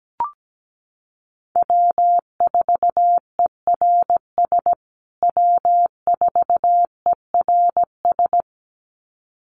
Morse code, heard as a keyed tone, about 17 wpm, 700 Hz. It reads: W4ERS W4ERS
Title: USA Call Signs Repeated 2x 17wpm